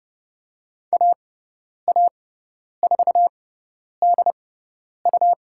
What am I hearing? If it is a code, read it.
Morse code, heard as a keyed tone, about 30 wpm, spaced as for 10 wpm, 700 Hz. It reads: AA4DU
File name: Lesson 18g - Letter m - Call Signs 30wpm